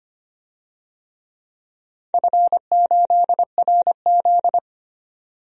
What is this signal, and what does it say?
Morse code, heard as a keyed tone, about 25 wpm, 700 Hz. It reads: F8RZ
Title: DXCC-100 Call Signs - 4 Characters - 25wpm